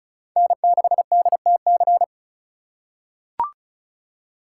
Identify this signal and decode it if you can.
Morse code, heard as a keyed tone, about 36 wpm, 700 Hz. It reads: N6DTC
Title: USA Call Signs - 5 Characters - 4x Character Spacing - 35wpm